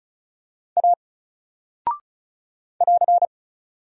Morse code, heard as a keyed tone, about 36 wpm, 700 Hz. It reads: A <AR>
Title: Single Character (Letter:Number:Punctuation:Prosign) 35wpm